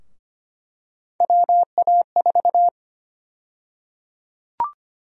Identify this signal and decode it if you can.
Morse code, heard as a keyed tone, about 25 wpm, 700 Hz. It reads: WA4
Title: USA Call Sign Prefixes - 25wpm